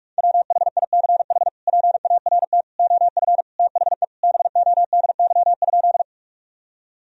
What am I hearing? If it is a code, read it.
Morse code, heard as a keyed tone, about 46 wpm, 700 Hz. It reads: WHICH PART OF THE BODY?
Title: Sentences from Top 1000 Words - Review Part 2 - 45wpm